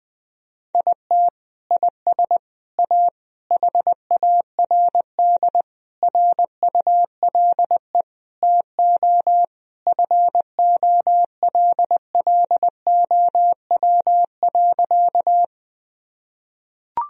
Morse code, heard as a keyed tone, about 21 wpm, 700 Hz. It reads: IT IS A HARD RULE TO FOLLOW.
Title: Sentences from Top 1000 Words - Review Part 2 - 20wpm